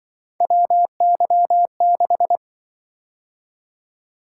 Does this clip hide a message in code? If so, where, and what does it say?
Yes: Morse code from 0.4 to 2.4 s, heard as a keyed tone, about 26 wpm, 700 Hz. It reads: WY6